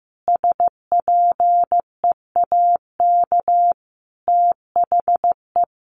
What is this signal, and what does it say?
Morse code, heard as a keyed tone, about 15 wpm, 700 Hz. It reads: SPEAK THE
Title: Sentences from Top 1100 Words 15wpm